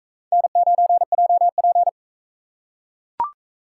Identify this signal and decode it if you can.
Morse code, heard as a keyed tone, about 42 wpm, 700 Hz. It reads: N9JP